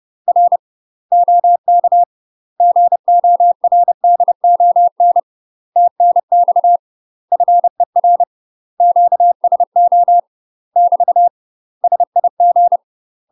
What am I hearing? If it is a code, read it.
Morse code, heard as a keyed tone, about 30 wpm, 700 Hz. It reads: R OK GORDON TNX FER QSO <BT> SIG
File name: UK MASTER QSOs TRAINING / 30WPM / UK MASTER QSOs 0051